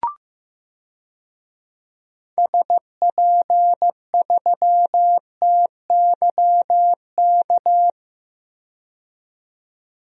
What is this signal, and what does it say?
Morse code, heard as a keyed tone, about 15 wpm, 700 Hz. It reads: SP3TYK
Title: DXCC-100 Call Signs - 6 Characters - 15wpm